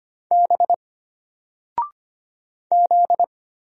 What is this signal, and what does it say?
Morse code, heard as a keyed tone, about 25 wpm, 700 Hz. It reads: B Z
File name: Single-Letter-Number-25wpm